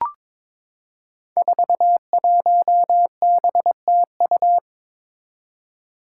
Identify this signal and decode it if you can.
Morse code, heard as a keyed tone, about 22 wpm, 700 Hz. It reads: 41BTU
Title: Five Challenging Letter-Number Combinations 22wpm